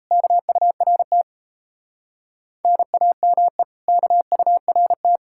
Morse code, heard as a keyed tone, about 35 wpm, 700 Hz. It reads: KURT NAME KURT